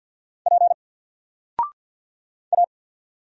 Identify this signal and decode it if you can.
Morse code, heard as a keyed tone, about 50 wpm, 700 Hz. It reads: P A
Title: Single Character (Letter:Number:Punctuation:Prosign) 50wpm